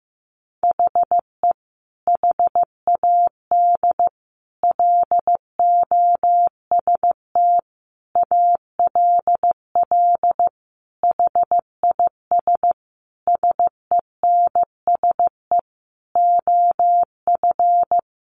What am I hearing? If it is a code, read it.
Morse code, heard as a keyed tone, about 15 wpm, 700 Hz. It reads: HE HAD LOST ALL HIS SENSE OF
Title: Sentences from Top 1400 Words 15wpm